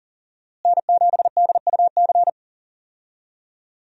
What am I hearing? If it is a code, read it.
Morse code, heard as a keyed tone, about 40 wpm, 700 Hz. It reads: N7DUC
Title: USA Call Signs - 5 Characters - Repeated 3x - 40wpm